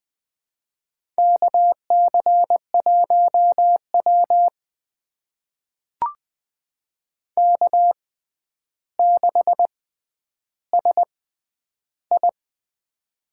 Morse code, heard as a keyed tone, about 20 wpm, spaced as for 10 wpm, 700 Hz. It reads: KC1W K6SI
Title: USA Call Signs - 4 Characters - 6x Character Spacing - 20wpm